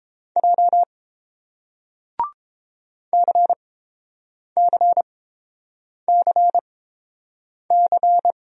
Morse code, heard as a keyed tone, about 30 wpm, spaced as for 7 wpm, 700 Hz. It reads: J CCCC